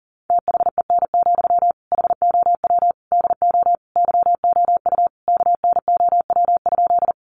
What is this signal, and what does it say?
Morse code, heard as a keyed tone, about 40 wpm, 700 Hz. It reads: THEN, HOW DO YOU KNOW?
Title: Sentences from Top 100 Words 40wpm